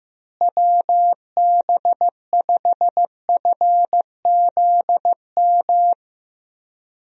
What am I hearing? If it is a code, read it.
Morse code, heard as a keyed tone, about 15 wpm, 700 Hz. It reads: WB5FZM
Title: USA Call Signs - 6 Characters - Repeated 3x - 15wpm